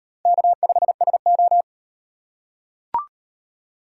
Morse code, heard as a keyed tone, about 38 wpm, 700 Hz. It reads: K5SO